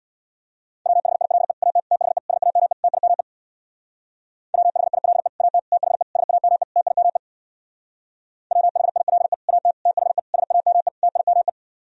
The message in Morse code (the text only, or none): WHILE AT THE SAME TIME WHILE AT THE SAME TIME WHILE AT THE SAME TIME